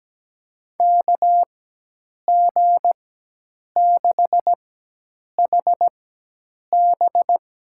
Morse code, heard as a keyed tone, about 17 wpm, spaced as for 8 wpm, 700 Hz. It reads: KG6HB
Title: USA Call Signs - 4x Character Spacing - 17wpm